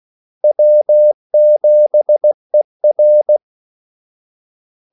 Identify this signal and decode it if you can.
Morse code, heard as a keyed tone, about 16 wpm, 600 Hz. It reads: W7ER
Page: Kaldesignaler 16 wpm